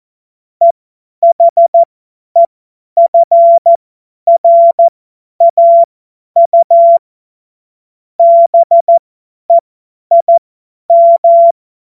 Morse code, heard as a keyed tone, about 14 wpm, spaced as for 10 wpm, 680 Hz. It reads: EHEFRAU BEIM
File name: Morsekurs / Witze / Witz 02 April 2024 50 F70